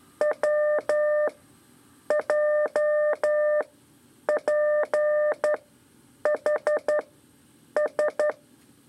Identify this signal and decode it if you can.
Morse code, heard as a keyed tone, about 11 wpm, spaced as for 7 wpm, 600 Hz. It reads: WJPHS
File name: 3 lezione CW 2 AIR